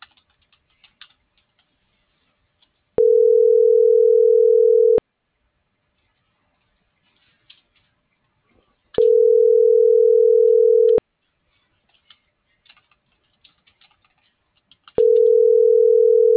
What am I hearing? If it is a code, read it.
Ringback tone (North America), 440 + 480 Hz.
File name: CENTRAL SOTOVA / out alejandro.rodriguez